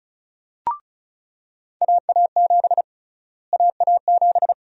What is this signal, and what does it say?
Morse code, heard as a keyed tone, about 34 wpm, 700 Hz. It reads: AA7 AA7
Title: USA Call Sign Prefixes - Repeated 2x - 35wpm